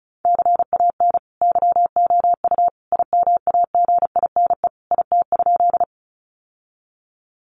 Morse code, heard as a keyed tone, about 34 wpm, 700 Hz. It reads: CAN YOU IMAGINE IT?
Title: Sentences from Top 1400 Words 35wpm